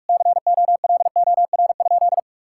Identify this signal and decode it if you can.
Morse code, heard as a keyed tone, about 41 wpm, 690 Hz. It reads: COLOR?
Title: Sentences from Top 1000 Words - Review Part 6 - 45wpm